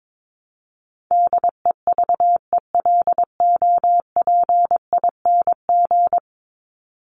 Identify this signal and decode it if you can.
Morse code, heard as a keyed tone, about 22 wpm, 700 Hz. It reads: DEVELOPING
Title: ING Suffix - Top 100 Words with Base Words 22wpm